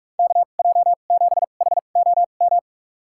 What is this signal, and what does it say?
Morse code, heard as a keyed tone, about 46 wpm, spaced as for 32 wpm, 700 Hz. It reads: KJ7HOM
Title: USA Call Signs - 6 Characters - 2x Character Spacing - 45wpm